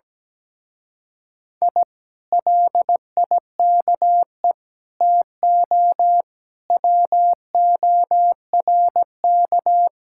Morse code, heard as a keyed tone, about 17 wpm, 700 Hz. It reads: I LIKE TO WORK
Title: Speed-Racing Common QSO Phrases Encore - 17wpm